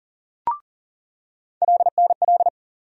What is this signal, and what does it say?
Morse code, heard as a keyed tone, about 40 wpm, 700 Hz. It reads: LNL
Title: USA Call Sign Postfixes - 3 Letters 40wpm